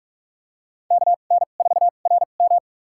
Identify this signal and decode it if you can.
Morse code, heard as a keyed tone, about 46 wpm, spaced as for 32 wpm, 700 Hz. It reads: KN4RM
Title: USA Call Signs - 5 Characters - 2x Character Spacing - 45wpm